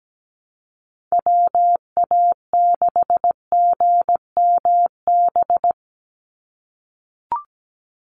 Morse code, heard as a keyed tone, about 17 wpm, 700 Hz. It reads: WA6GMB
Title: USA Call Signs 17wpm